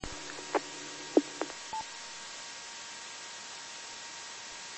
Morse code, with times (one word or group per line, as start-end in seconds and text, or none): none